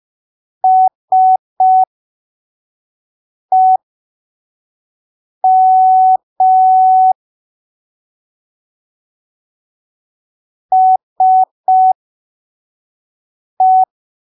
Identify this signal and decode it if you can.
Morse code, heard as a keyed tone, about 5 wpm, spaced as for 3 wpm, 750 Hz. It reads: SEM SE
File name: arquivos / texto / texto 21 05ppm